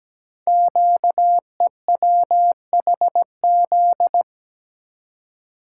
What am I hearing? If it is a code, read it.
Morse code, heard as a keyed tone, about 17 wpm, 700 Hz. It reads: QEWHZ
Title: Code Groups - 5 Letter:Numbers - 17wpm